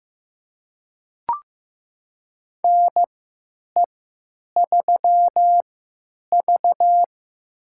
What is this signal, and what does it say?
Morse code, heard as a keyed tone, about 15 wpm, spaced as for 9 wpm, 700 Hz. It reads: NE3V
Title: USA Call Signs - 4 Characters - 3x Character Spacing - 15wpm